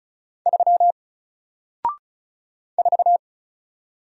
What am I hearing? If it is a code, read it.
Morse code, heard as a keyed tone, about 35 wpm, 700 Hz. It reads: 3 4
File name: Single-Letter-Number-35wpm